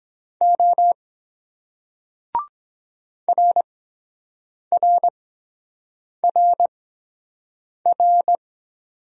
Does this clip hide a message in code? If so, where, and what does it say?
Yes: Morse code from 0.4 to 8.4 s, heard as a keyed tone, about 23 wpm, spaced as for 7 wpm, 700 Hz. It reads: O RRRR